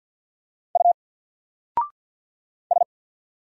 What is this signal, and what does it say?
Morse code, heard as a keyed tone, about 53 wpm, 700 Hz. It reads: U S
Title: Single Letter-Number 50wpm